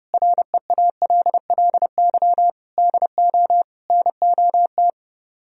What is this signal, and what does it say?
Morse code, heard as a keyed tone, about 31 wpm, 700 Hz. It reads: REALLY DO NOT